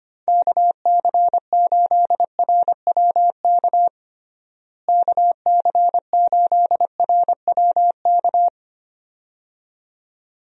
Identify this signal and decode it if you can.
Morse code, heard as a keyed tone, about 25 wpm, 700 Hz. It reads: KC8RWK KC8RWK
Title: USA Call Signs - 6 Characters - Repeated 2x - 25wpm